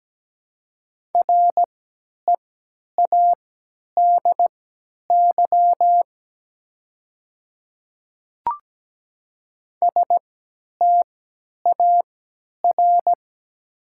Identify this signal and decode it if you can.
Morse code, heard as a keyed tone, about 17 wpm, spaced as for 10 wpm, 690 Hz. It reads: READY STAR
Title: 3x Character Spacing - Top 500 Words - 17wpm